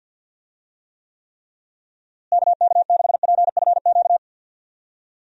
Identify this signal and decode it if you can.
Morse code, heard as a keyed tone, about 50 wpm, 700 Hz. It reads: KK6PFQ
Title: USA Call Signs - 6 Characters - Repeated 3x - 50wpm